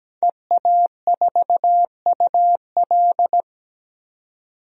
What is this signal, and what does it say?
Morse code, heard as a keyed tone, about 17 wpm, 700 Hz. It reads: EA4UL
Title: DXCC-100 Call Signs - 5 Characters - 17wpm